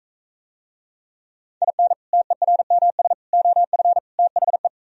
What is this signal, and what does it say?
Morse code, heard as a keyed tone, about 45 wpm, 700 Hz. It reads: IN TERMS OF THE